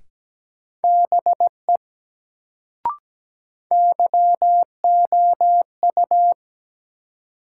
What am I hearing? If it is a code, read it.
Morse code, heard as a keyed tone, about 17 wpm, 700 Hz. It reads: BE YOU
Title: Sentences from Top 100 Words - New Word Drill - 17wpm